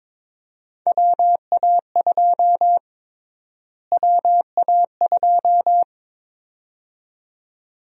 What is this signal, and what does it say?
Morse code, heard as a keyed tone, about 22 wpm, 700 Hz. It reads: WA2 WA2
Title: USA Call Sign Prefixes - Repeated 2x - 22wpm